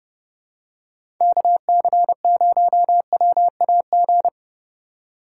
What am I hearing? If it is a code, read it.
Morse code, heard as a keyed tone, about 30 wpm, 700 Hz. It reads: KC0WAG
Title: USA Call Signs - 6 Characters - 2x Character Spacing - 30wpm